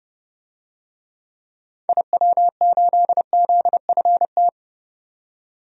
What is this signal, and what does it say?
Morse code, heard as a keyed tone, about 32 wpm, 700 Hz. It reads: IW8ZFT
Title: DXCC-100 Call Signs - 6 Characters - 30wpm